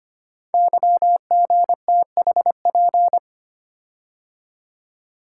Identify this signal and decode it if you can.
Morse code, heard as a keyed tone, about 25 wpm, 700 Hz. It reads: YGTHP
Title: Code Groups - 5 Letters - 25wpm